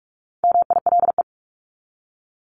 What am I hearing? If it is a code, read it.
Morse code, heard as a keyed tone, about 44 wpm, 700 Hz. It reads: MILE
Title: ES:S Suffix - Top 100 Words with Base Words 45wpm